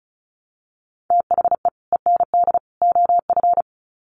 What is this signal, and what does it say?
Morse code, heard as a keyed tone, about 34 wpm, 700 Hz. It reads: THE END OF
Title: Sets of 3 Words 35wpm